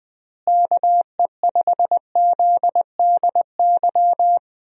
Morse code, heard as a keyed tone, about 20 wpm, 700 Hz. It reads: KE5ZDY
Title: USA Call Signs - 6 Characters - Repeated 3x - 20wpm